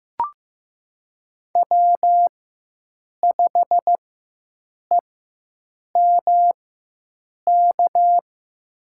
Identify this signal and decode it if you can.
Morse code, heard as a keyed tone, about 15 wpm, spaced as for 7 wpm, 700 Hz. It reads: W5EMK
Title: USA Call Signs - 5 Characters - 4x Character Spacing - 15wpm